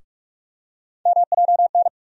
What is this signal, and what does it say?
Morse code, heard as a keyed tone, about 43 wpm, 700 Hz. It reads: MJN
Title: USA Call Sign Postfixes - 3 Letters 45wpm